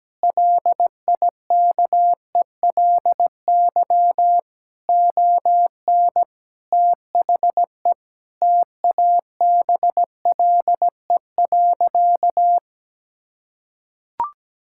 Morse code, heard as a keyed tone, about 17 wpm, 700 Hz. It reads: LIKELY ON THE TABLE.
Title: Sentences from Top 1000 Words - Review Part 5 - 17wpm